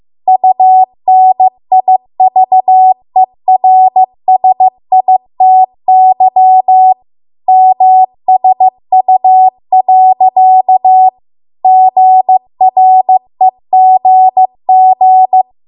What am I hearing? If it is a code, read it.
Morse code, heard as a keyed tone, about 15 wpm, 750 Hz. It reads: UNIVERSITY MSU. GREGG
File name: Archive / 15 WPM / 15WPM